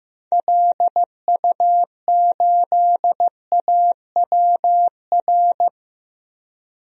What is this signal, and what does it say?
Morse code, heard as a keyed tone, about 15 wpm, 700 Hz. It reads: LU8AWR
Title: DXCC-100 Call Signs - 6 Characters - 15wpm